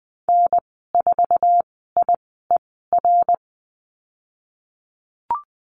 Morse code, heard as a keyed tone, about 20 wpm, spaced as for 14 wpm, 700 Hz. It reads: N4IER
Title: 062 Lesson 11g - Letter h - Call Signs 20wpm